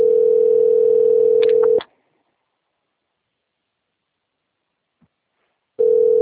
Ringback tone (North America), 440 + 480 Hz.